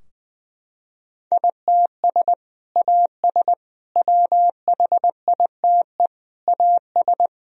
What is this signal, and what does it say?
Morse code, heard as a keyed tone, about 21 wpm, 700 Hz. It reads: ITS AS WHITE AS
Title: Sentences from Top 1000 Words - Review Part 3 - 20wpm